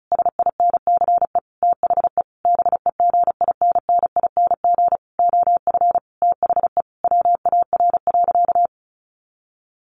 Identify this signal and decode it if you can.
Morse code, heard as a keyed tone, about 34 wpm, 700 Hz. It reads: SINCE THE BEGINNING OF THE WAR.
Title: Sentences from Top 1400 Words 35wpm